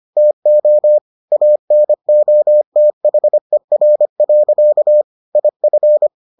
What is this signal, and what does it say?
Morse code, heard as a keyed tone, about 25 wpm, 600 Hz. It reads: TO ANOTHER. IF